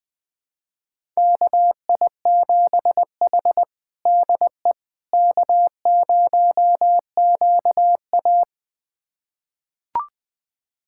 Morse code, heard as a keyed tone, about 20 wpm, 700 Hz. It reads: KI7H DE K0QA
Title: USA Callsign DE Callsign - 4 Characters - 20wpm